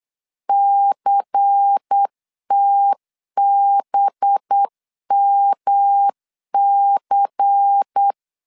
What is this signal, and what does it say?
Morse code, heard as a keyed tone, about 8 wpm, 800 Hz. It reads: CTBMC
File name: LEZIONI / 7 GF9Z / 7 GF 8wpm 3108